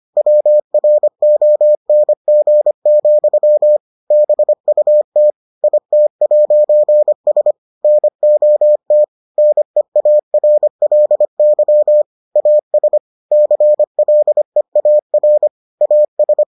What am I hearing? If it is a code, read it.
Morse code, heard as a keyed tone, about 25 wpm, 600 Hz. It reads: WRONG, BUT IT'S NOT NEARLY AS CLEAR AS